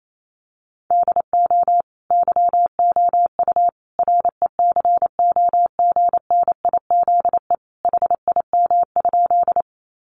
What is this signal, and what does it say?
Morse code, heard as a keyed tone, about 28 wpm, 700 Hz. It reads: DO YOU RECOGNIZE HIM?
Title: Sentences from Top 1400 Words 28wpm